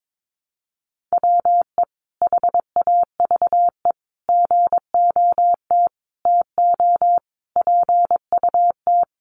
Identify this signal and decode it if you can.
Morse code, heard as a keyed tone, about 22 wpm, 700 Hz. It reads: WE HAVE GOT TO PUT
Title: Sentences from Top 1200 Words 22wpm